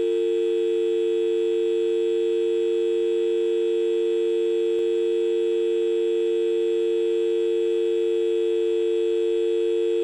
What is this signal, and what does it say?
Dial tone (UK), 350 + 450 Hz.